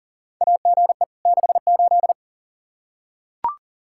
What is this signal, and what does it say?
Morse code, heard as a keyed tone, about 38 wpm, 700 Hz. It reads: AGE 68
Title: Practice QSO - Exchange 3 - More Info - 40wpm